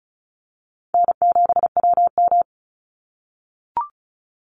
Morse code, heard as a keyed tone, about 36 wpm, 700 Hz. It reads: N7WM
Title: NA SOTA Chaser Callsigns 35wpm